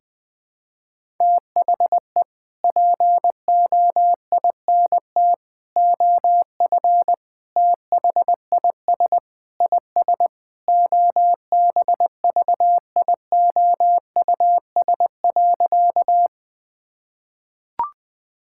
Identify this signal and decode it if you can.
Morse code, heard as a keyed tone, about 20 wpm, 700 Hz. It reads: THE POINT OF THIS IS OBVIOUS.
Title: Sentences from Top 1600 Words 20wpm